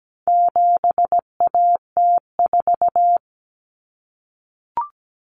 Morse code, heard as a keyed tone, about 17 wpm, 700 Hz. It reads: 7AT4
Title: Common Cut Numbers - 4-Digit 17wpm